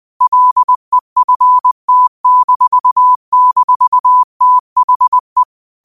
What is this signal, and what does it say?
Morse code, heard as a keyed tone, about 20 wpm, 1000 Hz. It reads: LEFT--THE
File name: qotdfiles / 20wpm